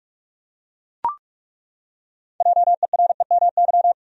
Morse code, heard as a keyed tone, about 41 wpm, 700 Hz. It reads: JEREMY
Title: USA Common Names - 45wpm